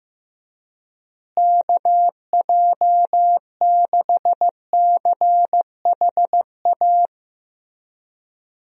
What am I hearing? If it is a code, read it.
Morse code, heard as a keyed tone, about 16 wpm, 700 Hz. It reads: KJ6CHA
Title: USA Call Signs - 6 Characters - Repeated 2x - 15wpm